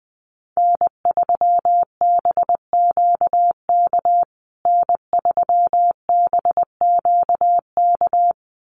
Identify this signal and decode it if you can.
Morse code, heard as a keyed tone, about 20 wpm, 700 Hz. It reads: N3BQK N3BQK
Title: USA Call Signs Repeated 2x 20wpm